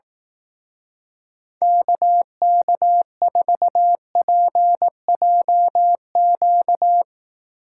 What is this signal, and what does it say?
Morse code, heard as a keyed tone, about 18 wpm, 700 Hz. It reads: KK4PJQ
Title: Speed Racing - USA 2x3 Call Signs 15wpm